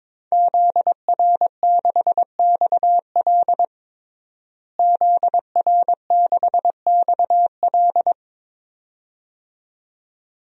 Morse code, heard as a keyed tone, about 22 wpm, 700 Hz. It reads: ZR6XL ZR6XL
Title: DXCC-100 Call Signs - 5 Characters - Repeated 2x - 22wpm